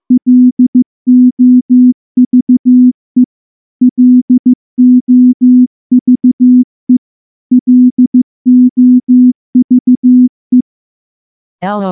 Morse code, heard as a keyed tone, about 15 wpm, 250 Hz. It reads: LOVE LOVE LOVE
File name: morse / morse-training-15WPM